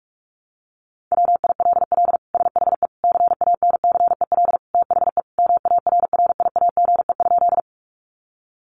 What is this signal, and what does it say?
Morse code, heard as a keyed tone, about 46 wpm, 700 Hz. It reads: WILL SHE CANCEL THE MARRIAGE?
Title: Sentences from Top 1200 Words 45wpm